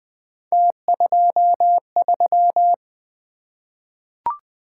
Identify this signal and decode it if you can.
Morse code, heard as a keyed tone, about 20 wpm, 700 Hz. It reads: T23
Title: Three Letter-Number Combinations - 20wpm